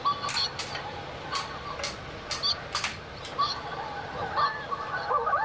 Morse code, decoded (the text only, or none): none